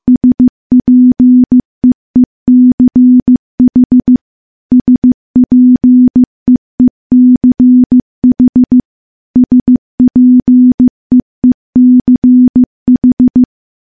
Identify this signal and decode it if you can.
Morse code, heard as a keyed tone, about 15 wpm, 250 Hz. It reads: SPEECH SPEECH SPEECH
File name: morse-training-15WPM